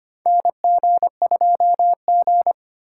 Morse code, heard as a keyed tone, about 25 wpm, 700 Hz. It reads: NG2G
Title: USA Call Signs - 4x Character Spacing - 25wpm